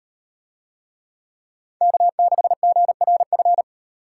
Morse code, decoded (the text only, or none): K6GRF